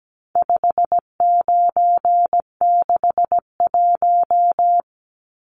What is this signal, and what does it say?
Morse code, heard as a keyed tone, about 17 wpm, 700 Hz. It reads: 5961
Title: Numbers - 4-Digit 17wpm